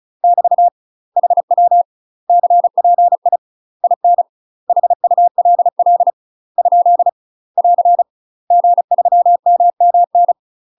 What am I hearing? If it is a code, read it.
Morse code, heard as a keyed tone, about 34 wpm, 700 Hz. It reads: <BT> HW CPI IN HULL ? <AR> G3MMN
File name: UK MASTER QSOs TRAINING / 35WPM / UK MASTER QSOs 0030